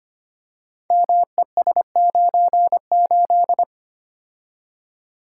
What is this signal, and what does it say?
Morse code, heard as a keyed tone, about 25 wpm, 700 Hz. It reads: MES98
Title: Code Groups - 5 Letter:Numbers - 25wpm